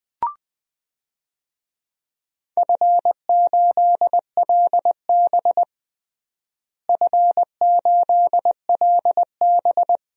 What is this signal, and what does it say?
Morse code, heard as a keyed tone, about 21 wpm, 700 Hz. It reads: F8LB F8LB
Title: DXCC-100 Call Signs - 4 Characters - Repeated 3x - 20wpm